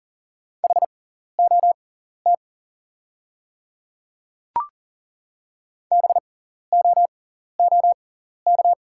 Morse code, heard as a keyed tone, about 40 wpm, spaced as for 14 wpm, 690 Hz. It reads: HOT BOOK
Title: 6x Character Spacing - Top 500 Words - 40wpm